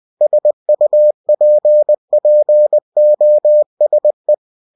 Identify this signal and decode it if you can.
Morse code, heard as a keyed tone, about 20 wpm, 600 Hz. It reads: SUPPOSE
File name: cw-books / winnie-the-pooh / winnie-20wpm-page-0041